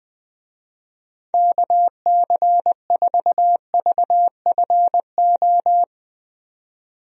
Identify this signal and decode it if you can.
Morse code, heard as a keyed tone, about 20 wpm, 700 Hz. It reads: KC4VFO
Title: USA Call Signs - 6 Characters - 20wpm